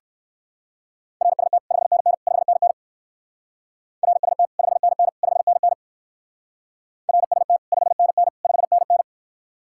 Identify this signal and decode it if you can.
Morse code, heard as a keyed tone, about 56 wpm, 700 Hz. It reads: RST 5NN 5NN RST 5NN 5NN RST 5NN 5NN
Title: Speed-Racing Common QSO Phrases - 45wpm